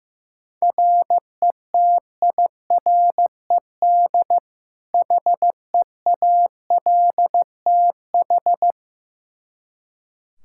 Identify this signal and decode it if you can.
Morse code, heard as a keyed tone, about 15 wpm, 700 Hz. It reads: RETIRED HEALTH
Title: Practice QSO - Exchange 3 - More Info - 15wpm